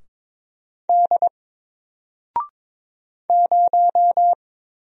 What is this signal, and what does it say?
Morse code, heard as a keyed tone, about 22 wpm, 700 Hz. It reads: D 0
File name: Single-Letter-Number-22wpm